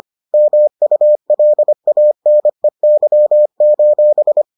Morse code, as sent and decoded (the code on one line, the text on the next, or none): -- ..- .-.. .- -. -.-- ---...
MULANEY: